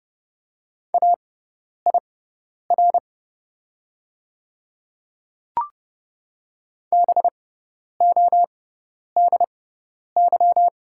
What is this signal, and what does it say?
Morse code, heard as a keyed tone, about 30 wpm, spaced as for 10 wpm, 700 Hz. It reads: AIR BODY
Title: 6x Character Spacing - Top 500 Words - 30wpm